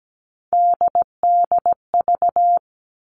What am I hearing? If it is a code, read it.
Morse code, heard as a keyed tone, about 17 wpm, 700 Hz. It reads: DDV
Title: All Cut Numbers - 3-Digit 17wpm